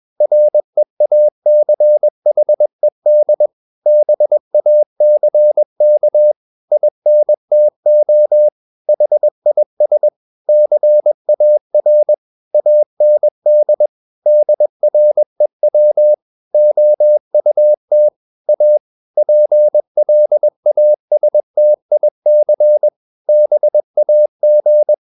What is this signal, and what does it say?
Morse code, heard as a keyed tone, about 21 wpm, 600 Hz. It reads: REACHED BACK INTO HIS CAR AND DREW OUT A PLASTIC BAG
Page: CW med Gnister